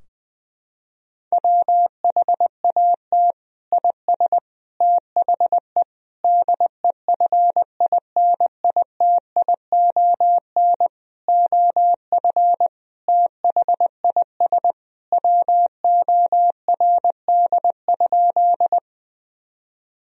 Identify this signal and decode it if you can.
Morse code, heard as a keyed tone, about 20 wpm, 700 Hz. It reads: WHAT IS THE DEFINITION OF THIS WORD?
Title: Sentences from Top 1000 Words - Review Part 4 - 20wpm